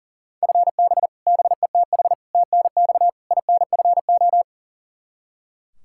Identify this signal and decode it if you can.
Morse code, heard as a keyed tone, about 40 wpm, 700 Hz. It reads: FB BETH TNX INFO
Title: Practice QSO - Exchange 3 - More Info - 40wpm